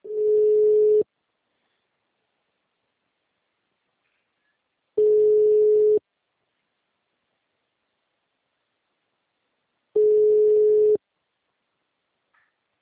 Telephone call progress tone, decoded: ringback tone

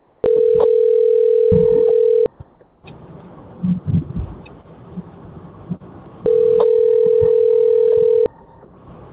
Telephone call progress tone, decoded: ringback tone

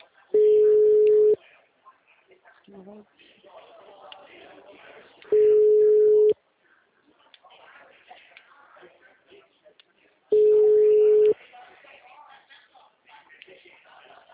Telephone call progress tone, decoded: ringback tone

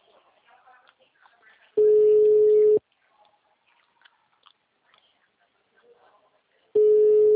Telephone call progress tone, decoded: ringback tone

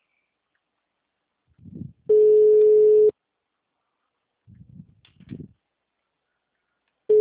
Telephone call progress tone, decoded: ringback tone

